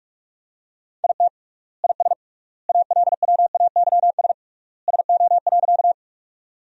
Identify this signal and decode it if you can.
Morse code, heard as a keyed tone, about 44 wpm, 700 Hz. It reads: IT IS ALWAYS SO.
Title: 3x Word-Spacing - Sentences from Top 300 Words - 45wpm